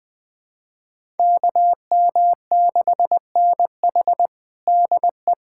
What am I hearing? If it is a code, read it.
Morse code, heard as a keyed tone, about 20 wpm, 700 Hz. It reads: KM6NH DE
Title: USA Callsign DE Callsign - 5 Characters - 20wpm